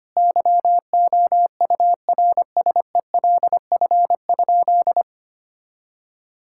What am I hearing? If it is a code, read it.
Morse code, heard as a keyed tone, about 25 wpm, 700 Hz. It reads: YOURSELF?
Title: Sentences from Top 1600 Words 25wpm